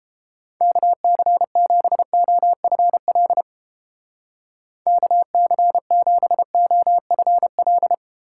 Morse code, heard as a keyed tone, about 29 wpm, 700 Hz. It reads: KC7OFL KC7OFL